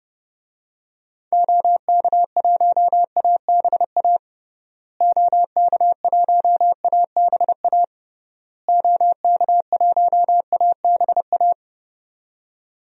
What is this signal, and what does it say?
Morse code, heard as a keyed tone, about 32 wpm, 700 Hz. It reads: OK1ABA OK1ABA OK1ABA
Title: DXCC-100 Call Signs - 6 Characters - Repeated 3x - 30wpm